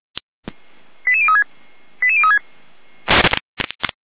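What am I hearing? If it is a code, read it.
Five-tone selective call (ZVEI), heard as quick runs of five tones, tones of about 70 ms. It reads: 88036 88036
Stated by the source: ZVEI five-tone selective call received by SDR, from the Feuerwehr Orlamünde audio archive.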